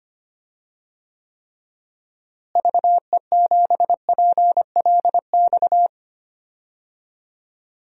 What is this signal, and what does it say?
Morse code, heard as a keyed tone, about 25 wpm, 700 Hz. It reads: VE7PLX
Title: DXCC-100 Call Signs - 6 Characters - Repeated 2x - 25wpm